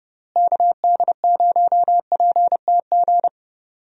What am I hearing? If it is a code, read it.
Morse code, heard as a keyed tone, about 30 wpm, 700 Hz. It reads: KD0PTG
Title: USA Call Signs - 6 Characters - Repeated 3x - 30wpm